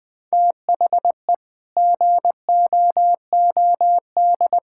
Morse code, heard as a keyed tone, about 20 wpm, 700 Hz. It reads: THE GOOD
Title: Sets of 6 Words - 20wpm